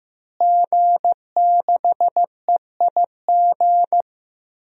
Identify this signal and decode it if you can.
Morse code, heard as a keyed tone, about 15 wpm, 700 Hz. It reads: G6EIG
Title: DXCC-100 Call Signs - 5 Characters - Repeated 3x - 15wpm